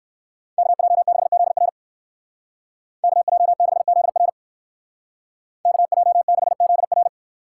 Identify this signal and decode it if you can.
Morse code, heard as a keyed tone, about 59 wpm, 700 Hz. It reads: KJ6ZR KJ6ZR KJ6ZR